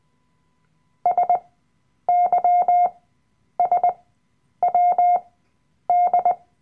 Morse code, heard as a keyed tone, about 20 wpm, spaced as for 9 wpm, 710 Hz. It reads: SYSWD
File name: Workshop Dateien / CW-Online-Kurs / Lektion 007